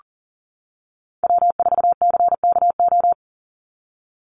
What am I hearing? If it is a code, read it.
Morse code, heard as a keyed tone, about 40 wpm, 700 Hz. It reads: W4CKO
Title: USA Call Signs 40wpm